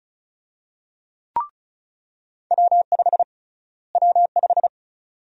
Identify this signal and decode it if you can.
Morse code, heard as a keyed tone, about 36 wpm, 700 Hz. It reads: W5 W5
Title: USA Call Sign Prefixes - Repeated 2x - 35wpm